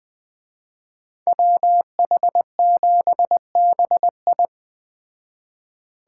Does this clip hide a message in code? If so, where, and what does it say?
Yes: Morse code from 1.3 to 4.5 s, heard as a keyed tone, about 20 wpm, 700 Hz. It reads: WH7BI